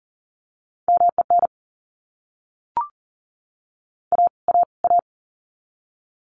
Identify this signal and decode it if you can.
Morse code, heard as a keyed tone, about 40 wpm, 700 Hz. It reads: MEN A A A
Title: Top 500 Words Repeated 40wpm